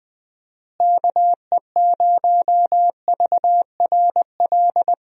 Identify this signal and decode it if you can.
Morse code, heard as a keyed tone, about 20 wpm, 700 Hz. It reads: KE0VRL
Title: USA Call Signs - 6 Characters - Repeated 3x - 20wpm